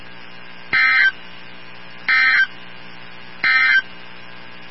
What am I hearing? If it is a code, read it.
Emergency Alert System (SAME) tones, heard as harsh buzzing bursts. It is heard from 0.6 to 3.9 s.
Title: NOAA Weather Radio 162.550